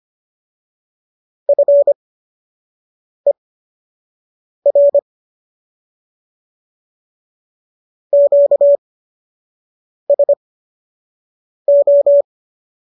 Morse code, heard as a keyed tone, about 25 wpm, spaced as for 6 wpm, 570 Hz. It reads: FER QSO
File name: qso002 06